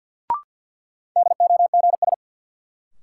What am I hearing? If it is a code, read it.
Morse code, heard as a keyed tone, about 52 wpm, 700 Hz. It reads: DOGS
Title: Sentences from Top 1600 Words - New Word Drill - 50wpm